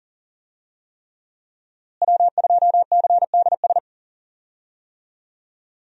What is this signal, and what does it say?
Morse code, heard as a keyed tone, about 37 wpm, 700 Hz. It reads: W2CDS
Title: USA Call Signs - 5 Characters - Repeated 3x - 40wpm